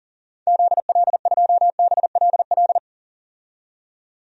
Morse code, heard as a keyed tone, about 37 wpm, 700 Hz. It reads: ZL2BLL
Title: DXCC-100 Call Signs - 6 Characters - 40wpm